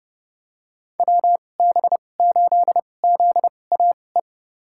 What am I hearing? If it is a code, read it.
Morse code, heard as a keyed tone, about 30 wpm, spaced as for 22 wpm, 700 Hz. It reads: WB8ZAE
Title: USA Call Signs - 2x Character Spacing - 30wpm